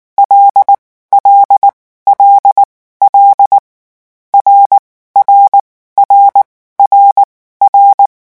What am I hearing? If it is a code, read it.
Morse code, heard as a keyed tone, about 19 wpm, spaced as for 14 wpm, 800 Hz. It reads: L L L L RR RRR
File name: morse mp3 / lecon6-vitesse 18